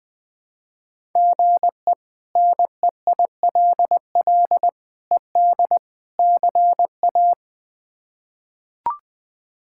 Morse code, heard as a keyed tone, about 20 wpm, 700 Hz. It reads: GE NEILL ED CA
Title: K1USN SST QSO Practice - 20wpm